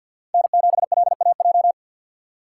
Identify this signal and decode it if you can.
Morse code, heard as a keyed tone, about 53 wpm, 700 Hz. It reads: N7LAJ
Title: USA Call Signs - 5 Characters - 3x Character Spacing - 50wpm